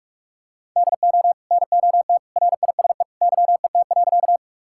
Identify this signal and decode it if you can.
Morse code, heard as a keyed tone, about 46 wpm, 700 Hz. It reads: DO NOT RISE YET.
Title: Sentences from Top 1000 Words - Review Part 6 - 45wpm